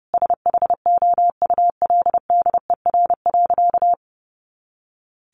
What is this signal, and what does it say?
Morse code, heard as a keyed tone, about 30 wpm, 700 Hz. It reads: SHOULDER.
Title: Additional Easy Phrases 30wpm